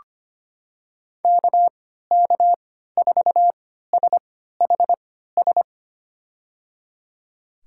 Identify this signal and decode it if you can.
Morse code, heard as a keyed tone, about 25 wpm, spaced as for 14 wpm, 700 Hz. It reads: KK4SHS
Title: USA Call Signs - 6 Characters - 3x Character Spacing - 25wpm